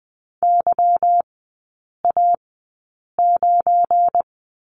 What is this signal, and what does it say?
Morse code, heard as a keyed tone, about 20 wpm, spaced as for 8 wpm, 690 Hz. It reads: YA9
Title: 172 Lesson 28e - Number 9 - Three Characters 20wpm